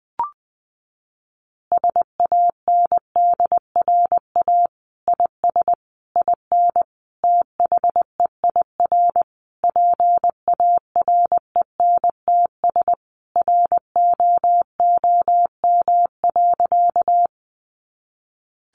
Morse code, heard as a keyed tone, about 20 wpm, 700 Hz. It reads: SANDRA IS IN THEIR PARENTS ROOM.
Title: Easy Phrases 20wpm